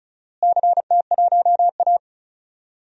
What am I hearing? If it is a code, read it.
Morse code, heard as a keyed tone, about 36 wpm, 700 Hz. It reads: CT1A
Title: DXCC-100 Call Signs - 4 Characters - Repeated 3x - 35wpm